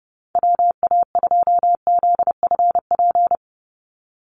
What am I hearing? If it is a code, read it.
Morse code, heard as a keyed tone, about 30 wpm, 700 Hz. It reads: WA2ZFP
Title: USA Call Signs 30wpm